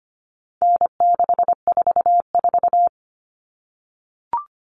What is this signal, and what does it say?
Morse code, heard as a keyed tone, about 25 wpm, 700 Hz. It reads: N644
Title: Common Cut Numbers - 4-Digit 25wpm